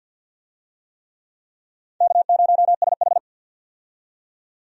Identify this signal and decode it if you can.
Morse code, heard as a keyed tone, about 51 wpm, 700 Hz. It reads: K0SH
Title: USA Call Signs - 4 Characters - 50wpm